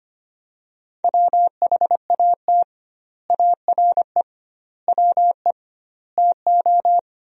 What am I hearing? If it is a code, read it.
Morse code, heard as a keyed tone, about 25 wpm, 700 Hz. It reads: WHAT ARE WE TO
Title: 2x Word-Spacing - Sentences from Top 100 Words - 25wpm